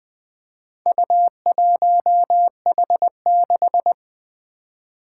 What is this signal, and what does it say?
Morse code, heard as a keyed tone, about 20 wpm, 700 Hz. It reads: U1H6
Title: Four Challenging Letter-Number Combinations 20wpm